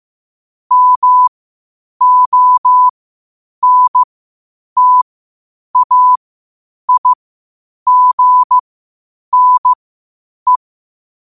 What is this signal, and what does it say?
Morse code, heard as a keyed tone, about 15 wpm, spaced as for 8 wpm, 1000 Hz. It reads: MONTAIGNE
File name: qotdfiles / 8wpm